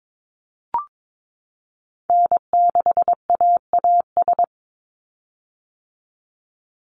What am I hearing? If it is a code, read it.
Morse code, heard as a keyed tone, about 22 wpm, 700 Hz. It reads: N6AAS
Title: USA Call Signs 22wpm